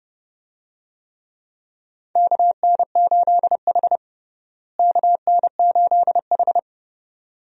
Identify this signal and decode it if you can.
Morse code, heard as a keyed tone, about 30 wpm, 700 Hz. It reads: KN8H KN8H
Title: USA Call Signs - 4 Characters - Repeated 2x - 30wpm